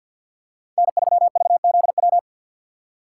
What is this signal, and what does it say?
Morse code, heard as a keyed tone, about 48 wpm, 700 Hz. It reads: N3VZW
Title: USA Call Signs - 5 Characters - 4x Character Spacing - 50wpm